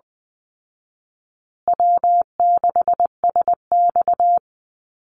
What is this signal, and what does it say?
Morse code, heard as a keyed tone, about 20 wpm, 700 Hz. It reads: W6SX
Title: Call Signs Repeated 3x - 20wpm